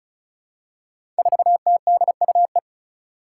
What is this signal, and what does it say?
Morse code, heard as a keyed tone, about 36 wpm, 700 Hz. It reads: 4TDUE
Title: Five Challenging Letter-Number Combinations 35wpm